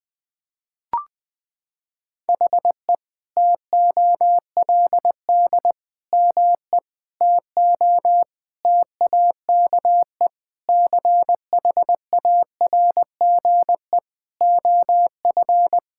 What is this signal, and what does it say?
Morse code, heard as a keyed tone, about 20 wpm, 700 Hz. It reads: HE TOLD ME TO TAKE CHARGE OF
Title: Sentences from Top 1000 Words - Review Part 6 - 20wpm